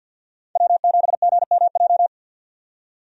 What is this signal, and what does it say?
Morse code, heard as a keyed tone, about 51 wpm, 700 Hz. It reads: W7GMJ
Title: USA Call Signs - 5 Characters - 50wpm